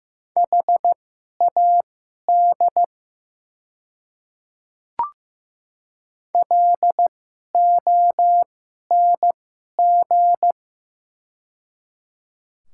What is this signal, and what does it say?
Morse code, heard as a keyed tone, about 15 wpm, spaced as for 11 wpm, 700 Hz. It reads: HAD LONG
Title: 2x Character Spacing - Top 100 Words - 15wpm